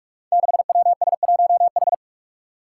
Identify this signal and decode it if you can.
Morse code, heard as a keyed tone, about 44 wpm, 700 Hz. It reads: 6WS1H
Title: Five Challenging Letter-Number Combinations 45wpm